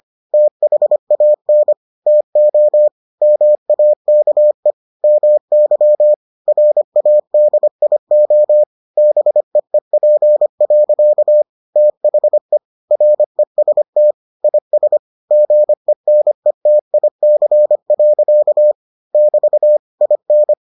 Morse code, heard as a keyed tone, about 25 wpm, 600 Hz. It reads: THAN TO MAKE MY RADIO BEEP. THE REST IS GENETIC. <BT> IN